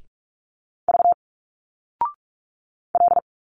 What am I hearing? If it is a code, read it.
Morse code, heard as a keyed tone, about 45 wpm, 700 Hz. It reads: V L
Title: Single Letters 45wpm